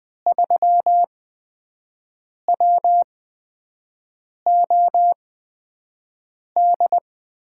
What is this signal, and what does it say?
Morse code, heard as a keyed tone, about 20 wpm, spaced as for 5 wpm, 700 Hz. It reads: 3WOD
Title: USA Call Signs - 5 Characters - 8x Character Spacing - 20wpm